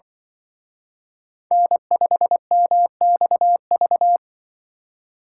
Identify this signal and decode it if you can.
Morse code, heard as a keyed tone, about 23 wpm, 700 Hz. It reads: N5MXV
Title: Speed Racing - USA 1x3 Call Signs 20wpm